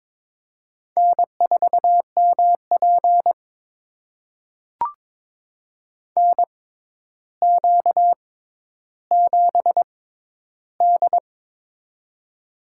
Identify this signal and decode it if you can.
Morse code, heard as a keyed tone, about 22 wpm, spaced as for 11 wpm, 700 Hz. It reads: N4MP NQ7D
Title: USA Call Signs - 4 Characters - 6x Character Spacing - 22wpm